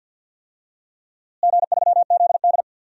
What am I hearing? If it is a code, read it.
Morse code, heard as a keyed tone, about 48 wpm, 700 Hz. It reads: G3ZD
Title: DXCC-100 Call Signs - 4 Characters - Repeated 3x - 50wpm